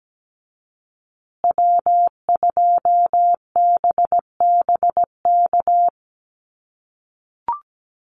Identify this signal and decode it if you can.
Morse code, heard as a keyed tone, about 17 wpm, 700 Hz. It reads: W2BBK
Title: USA Call Signs Repeated 2x 17wpm